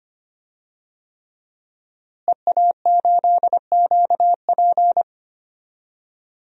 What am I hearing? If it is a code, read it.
Morse code, heard as a keyed tone, about 25 wpm, 700 Hz. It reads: EA8QP
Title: DXCC-100 Call Signs - 5 Characters - Repeated 3x - 25wpm